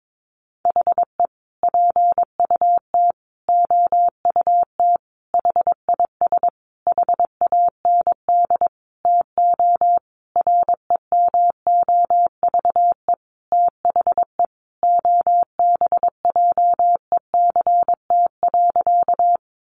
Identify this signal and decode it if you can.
Morse code, heard as a keyed tone, about 22 wpm, 700 Hz. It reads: HE PUT OUT HIS HAND TO REMOVE THE OBJECT.
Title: Sentences from Top 800 Words 22wpm